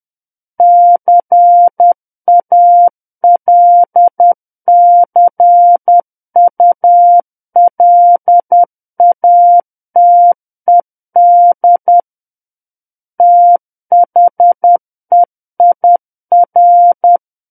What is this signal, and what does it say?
Morse code, heard as a keyed tone, about 10 wpm, 700 Hz. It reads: CALCULATED THEIR
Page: War of the Worlds - 01-Chapter 1 - 10 WPM